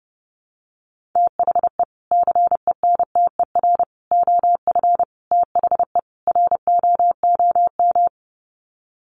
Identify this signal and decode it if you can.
Morse code, heard as a keyed tone, about 30 wpm, 700 Hz. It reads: THE CENTER OF THE ROOM
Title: Sets of 5 Words Encore 30wpm